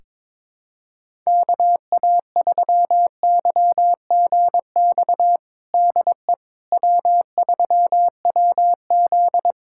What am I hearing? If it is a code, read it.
Morse code, heard as a keyed tone, about 22 wpm, 700 Hz. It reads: KA3YGX DE W3WZ
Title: USA Callsign DE Callsign 22wpm